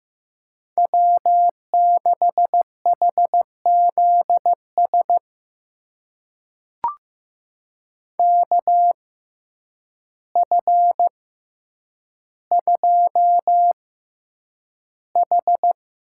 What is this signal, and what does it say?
Morse code, heard as a keyed tone, about 15 wpm, spaced as for 8 wpm, 700 Hz. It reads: W6HZS KF2H
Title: USA Call Signs - 6x Character Spacing - 15wpm